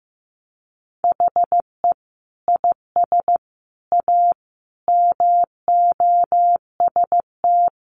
Morse code, heard as a keyed tone, about 15 wpm, 700 Hz. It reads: HE IS A MOST
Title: Sentences from Top 1000 Words 15wpm